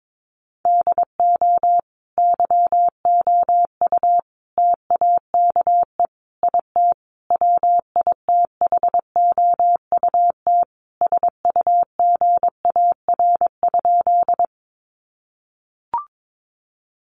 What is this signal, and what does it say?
Morse code, heard as a keyed tone, about 22 wpm, 700 Hz. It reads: DO YOU TAKE IT WITHOUT SUGAR?
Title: Sentences from Top 1200 Words 22wpm